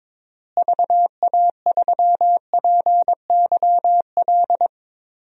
Morse code, heard as a keyed tone, about 22 wpm, 700 Hz. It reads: VA3PYL
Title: DXCC-100 Call Signs - 6 Characters - Repeated 3x - 22wpm